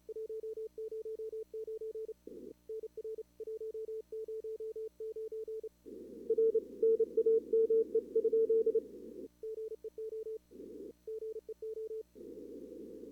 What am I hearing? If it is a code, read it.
Morse code, heard as a keyed tone, about 32 wpm, 440 Hz. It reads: 109 NR 109 R NAME? GEO GEO
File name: CWSprint / Feb2022 / k5tr-feb-na-sprint-cw-k5ot -0000